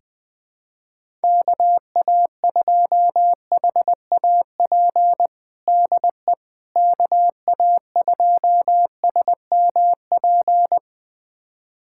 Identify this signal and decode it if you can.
Morse code, heard as a keyed tone, about 20 wpm, 700 Hz. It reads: KA2HAP DE KA2SMP
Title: USA Callsign DE Callsign - 6 Characters - 20wpm